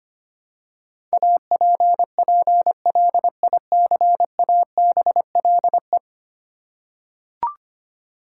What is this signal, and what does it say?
Morse code, heard as a keyed tone, about 25 wpm, 700 Hz. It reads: APPLICABLE
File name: Ten Letter Words - 25wpm